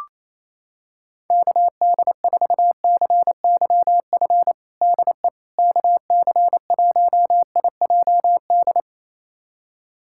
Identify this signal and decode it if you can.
Morse code, heard as a keyed tone, about 27 wpm, 700 Hz. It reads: KD4CYF DE KC1IJD
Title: USA Callsign DE Callsign - 6 Characters - 28wpm